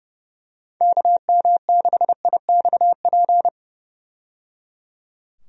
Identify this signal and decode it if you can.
Morse code, heard as a keyed tone, about 30 wpm, 700 Hz. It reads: KM6IXP
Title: USA Call Signs - 6 Characters - 30wpm